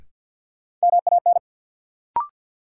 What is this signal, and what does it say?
Morse code, heard as a keyed tone, about 53 wpm, 700 Hz. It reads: MAN
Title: Sentences from Top 200 Words - New Word Drill - 50wpm